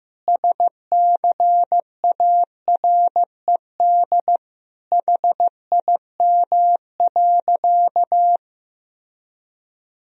Morse code, heard as a keyed tone, about 16 wpm, 700 Hz. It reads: SCARED HIM.
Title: Sentences from Top 1000 Words - Review Part 4 - 15wpm